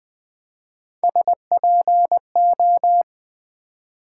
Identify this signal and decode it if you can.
Morse code, heard as a keyed tone, about 20 wpm, 700 Hz. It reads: SPO
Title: Build A Word - 3 Letters At A Time - 9-Letter Words - 20wpm